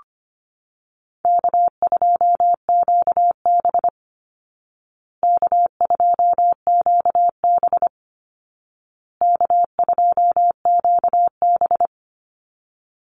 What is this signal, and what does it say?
Morse code, heard as a keyed tone, about 25 wpm, 700 Hz. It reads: K2QB K2QB K2QB
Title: Call Signs Repeated 2x - 25wpm